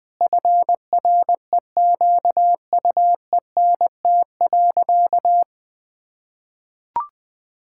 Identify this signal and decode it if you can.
Morse code, heard as a keyed tone, about 20 wpm, 700 Hz. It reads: FREQUENT.
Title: Sentences with 6-Word Phrases - 20wpm